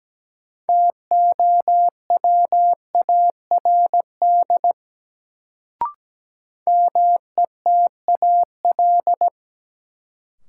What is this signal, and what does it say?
Morse code, heard as a keyed tone, about 17 wpm, 700 Hz. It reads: TOWARD METAL
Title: Sentences from Top 1000 Words - Review Part 6 - New Word Drill - 17wpm